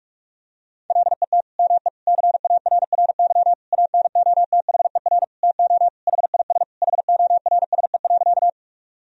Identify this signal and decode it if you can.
Morse code, heard as a keyed tone, about 46 wpm, 700 Hz. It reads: LET ME CARRY ANOTHER TO HIS HORSE.
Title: Sentences from Top 1000 Words - Review Part 1 - 45wpm